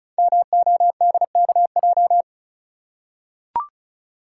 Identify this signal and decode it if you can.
Morse code, heard as a keyed tone, about 36 wpm, 700 Hz. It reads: MODKJ
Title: Code Groups - 5 Letter:Numbers - 35wpm